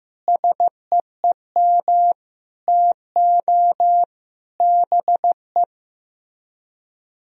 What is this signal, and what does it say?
Morse code, heard as a keyed tone, about 15 wpm, 700 Hz. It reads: SEEM TO BE
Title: Speed-Racing - Sets of 5 Words 15wpm